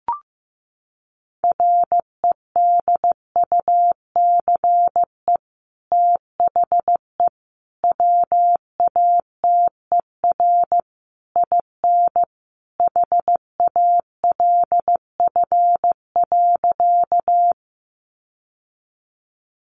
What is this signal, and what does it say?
Morse code, heard as a keyed tone, about 15 wpm, 700 Hz. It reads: REDUCE THE WATER IN HALF.
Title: Sentences from Top 1100 Words 15wpm